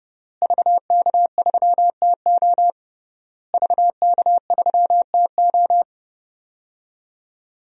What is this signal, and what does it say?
Morse code, heard as a keyed tone, about 30 wpm, 700 Hz. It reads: VK3TO VK3TO
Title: DXCC-100 Call Signs - 5 Characters - Repeated 3x - 30wpm